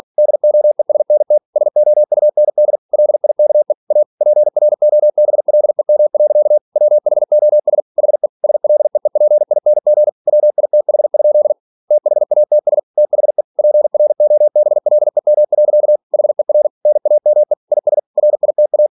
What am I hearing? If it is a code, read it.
Morse code, heard as a keyed tone, about 46 wpm, 600 Hz. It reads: DOESNT SOUND LIKE A PROBLEM. WHOS HE SLEEPING WITH? THATS THE PROBLEM. HER NAME IS RITA
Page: Never 47wpm